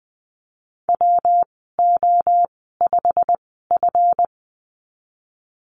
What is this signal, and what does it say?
Morse code, heard as a keyed tone, about 20 wpm, spaced as for 15 wpm, 700 Hz. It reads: WO5F
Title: 167 Lesson 27g - Number 7 - Call Signs 20wpm